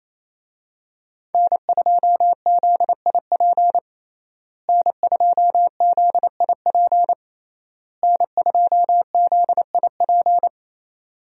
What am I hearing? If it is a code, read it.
Morse code, heard as a keyed tone, about 28 wpm, 700 Hz. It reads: N2ZIP N2ZIP N2ZIP
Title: USA Call Signs - 5 Characters - Repeated 3x - 28wpm